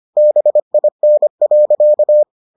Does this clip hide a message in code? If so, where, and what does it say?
Yes: Morse code from 0.2 to 2.2 s, heard as a keyed tone, about 25 wpm, 600 Hz. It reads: BIN.